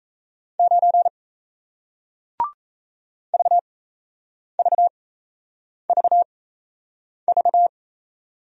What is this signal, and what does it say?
Morse code, heard as a keyed tone, about 38 wpm, spaced as for 8 wpm, 700 Hz. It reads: 9 VVVV